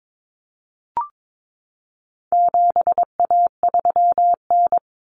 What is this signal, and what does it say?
Morse code, heard as a keyed tone, about 22 wpm, 700 Hz. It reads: 7A3N
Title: Common Cut Numbers - 4-Digit 22wpm